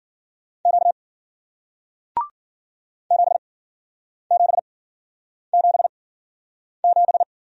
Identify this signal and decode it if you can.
Morse code, heard as a keyed tone, about 54 wpm, spaced as for 9 wpm, 700 Hz. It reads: Q 7777